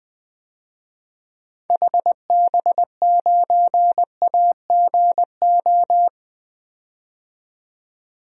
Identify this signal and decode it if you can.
Morse code, heard as a keyed tone, about 20 wpm, 700 Hz. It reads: HB9AGO
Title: DXCC-100 Call Signs - 6 Characters - 20wpm